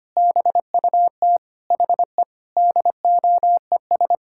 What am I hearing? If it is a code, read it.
Morse code, heard as a keyed tone, about 25 wpm, 700 Hz. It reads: BUT HE DOES
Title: Sentences from Top 1600 Words 25wpm